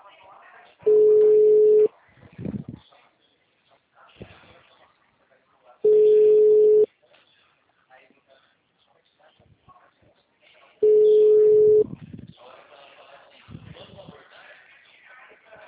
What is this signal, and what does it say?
Ringback tone (Europe, 425 Hz), with speech.
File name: out- -2017- - - .28965